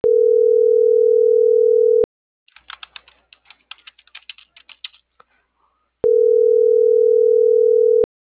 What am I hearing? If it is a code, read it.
Ringback tone (North America), 440 + 480 Hz.